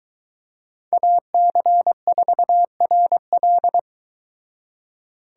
Morse code, heard as a keyed tone, about 23 wpm, 700 Hz. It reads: AC4RL